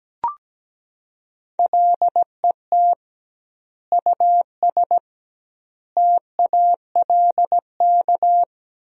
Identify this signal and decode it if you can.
Morse code, heard as a keyed tone, about 17 wpm, 700 Hz. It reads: LET US TALK
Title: 2x Word-Spacing - Sentences from Top 400 Words - 17wpm